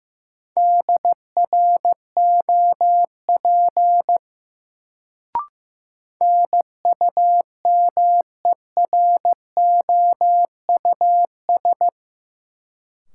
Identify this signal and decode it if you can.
Morse code, heard as a keyed tone, about 15 wpm, 700 Hz. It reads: DROP NUMEROUS
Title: Sentences from Top 1400 Words - New Word Drill - 15wpm